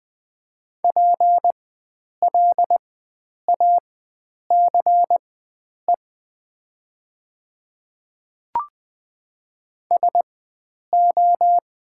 Morse code, heard as a keyed tone, about 20 wpm, spaced as for 9 wpm, 700 Hz. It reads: PLACE SO
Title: 4x Character Spacing - Top 300 Words - 20wpm